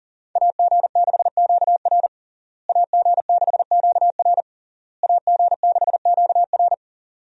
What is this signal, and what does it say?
Morse code, heard as a keyed tone, about 40 wpm, 690 Hz. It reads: AG6QR AG6QR AG6QR
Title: USA Call Signs - 5 Characters - Repeated 3x - 40wpm